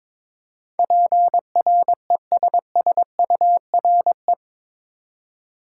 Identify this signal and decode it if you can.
Morse code, heard as a keyed tone, about 22 wpm, 700 Hz. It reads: PRESSURE
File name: practice / 22wpm / Eight Letter Words - 22wpm